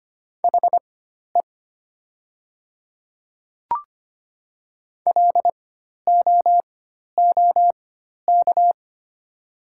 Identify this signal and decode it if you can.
Morse code, heard as a keyed tone, about 25 wpm, spaced as for 12 wpm, 700 Hz. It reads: HE LOOK
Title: 4x Character Spacing - Top 300 Words - 25wpm